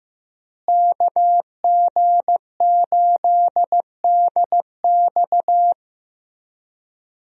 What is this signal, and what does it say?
Morse code, heard as a keyed tone, about 16 wpm, 700 Hz. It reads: KG8DX
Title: USA Call Signs - 5 Characters - Repeated 2x - 15wpm